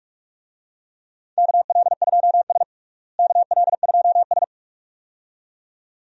Morse code, heard as a keyed tone, about 46 wpm, 700 Hz. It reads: KL2S KL2S
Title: USA Call Signs - 4 Characters - Repeated 2x - 45wpm